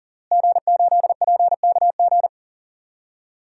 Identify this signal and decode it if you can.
Morse code, heard as a keyed tone, about 37 wpm, 700 Hz. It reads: G8PKG
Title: DXCC-100 Call Signs - 5 Characters - Repeated 3x - 40wpm